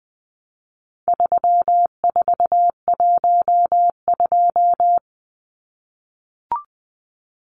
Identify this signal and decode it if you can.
Morse code, heard as a keyed tone, about 20 wpm, 700 Hz. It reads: 3412
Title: Numbers - 4-Digit 20wpm